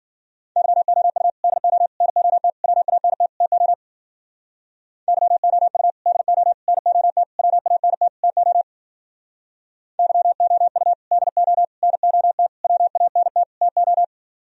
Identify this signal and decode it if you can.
Morse code, heard as a keyed tone, about 51 wpm, 700 Hz. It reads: YOU DO NOT WANT TO YOU DO NOT WANT TO YOU DO NOT WANT TO